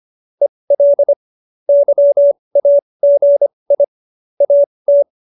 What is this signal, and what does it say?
Morse code, heard as a keyed tone, about 25 wpm, spaced as for 20 wpm, 570 Hz. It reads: EL YAGI AT
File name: QSO309 20